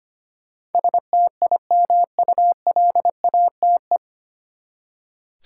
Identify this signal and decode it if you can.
Morse code, heard as a keyed tone, about 25 wpm, 700 Hz. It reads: STIMULATE
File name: Nine Letter Words - 25wpm